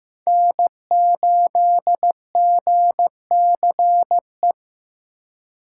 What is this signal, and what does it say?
Morse code, heard as a keyed tone, about 18 wpm, 700 Hz. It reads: N8GCE
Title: Speed Racing - USA 1x3 Call Signs 15wpm